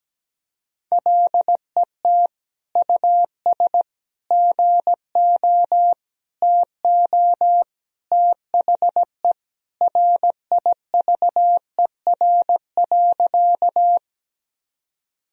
Morse code, heard as a keyed tone, about 17 wpm, 700 Hz. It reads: LET US GO TO THE RIVER.
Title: Sentences from Top 1000 Words - Review Part 1 - 17wpm